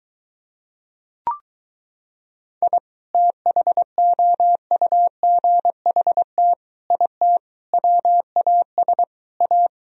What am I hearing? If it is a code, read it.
Morse code, heard as a keyed tone, about 23 wpm, 700 Hz. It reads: I THOUGHT IT WAS A